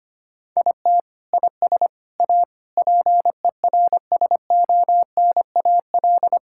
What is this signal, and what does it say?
Morse code, heard as a keyed tone, about 25 wpm, 700 Hz. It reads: IT IS A PERSONAL
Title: Sentences from Top 1000 Words - Review Part 5 - 25wpm